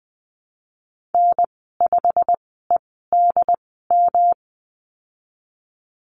Morse code, heard as a keyed tone, about 20 wpm, spaced as for 15 wpm, 700 Hz. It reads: N5EDM
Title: 122 Lesson 20g - Number 3 - Call Signs 20wpm